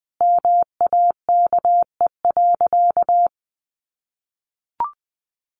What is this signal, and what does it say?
Morse code, heard as a keyed tone, about 20 wpm, 700 Hz. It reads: MAKE.
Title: Sentences from Top 1000 Words 20wpm